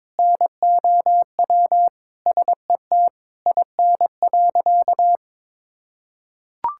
Morse code, heard as a keyed tone, about 22 wpm, 700 Hz. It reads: NOW SET IN.
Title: Sentences from Top 1000 Words - Review Part 5 - 22wpm